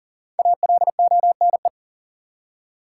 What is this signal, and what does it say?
Morse code, heard as a keyed tone, about 40 wpm, 700 Hz. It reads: ALONE
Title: Five Letter Words - 40wpm